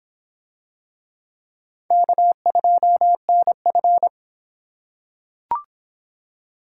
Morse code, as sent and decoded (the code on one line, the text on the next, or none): -.- ..--- -. ..-.
K2NF